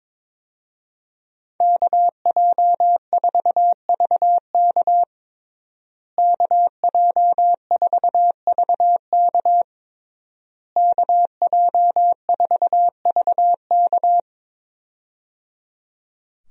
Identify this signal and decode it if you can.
Morse code, heard as a keyed tone, about 21 wpm, 700 Hz. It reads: KJ4VK KJ4VK KJ4VK
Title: USA Call Signs - 5 Characters - Repeated 3x - 22wpm